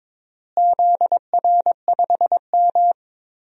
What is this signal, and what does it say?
Morse code, heard as a keyed tone, about 22 wpm, 700 Hz. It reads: ZR5M
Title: DXCC-100 Call Signs - 4 Characters - Repeated 3x - 22wpm